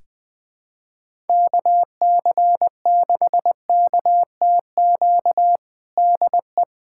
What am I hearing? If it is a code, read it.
Morse code, heard as a keyed tone, about 20 wpm, 700 Hz. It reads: KC6KTQ DE
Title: USA Callsign DE Callsign 20wpm